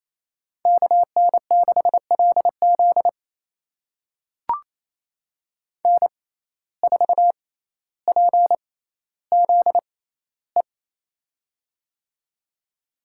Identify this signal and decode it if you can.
Morse code, heard as a keyed tone, about 28 wpm, spaced as for 14 wpm, 700 Hz. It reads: KN6LZ N4PZE
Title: USA Call Signs - 5 Characters - 6x Character Spacing - 28wpm